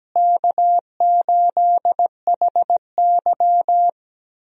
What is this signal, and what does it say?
Morse code, heard as a keyed tone, about 17 wpm, 700 Hz. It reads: K8HY
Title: USA Call Signs - 4 Characters - Repeated 2x - 17wpm